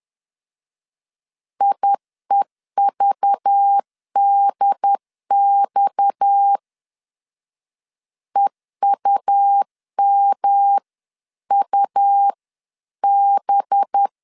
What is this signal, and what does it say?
Morse code, heard as a keyed tone, about 10 wpm, 800 Hz. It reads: IEVDX EUMUB
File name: CW / DET / 1 6 1500C S B 10wpm 4137